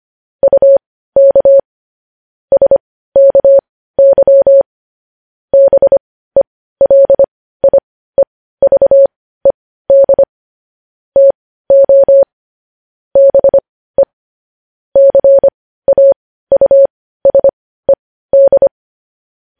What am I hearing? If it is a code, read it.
Morse code, heard as a keyed tone, about 25 wpm, spaced as for 15 wpm, 550 Hz. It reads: UK SKY BELIEVED TO BE CAUSED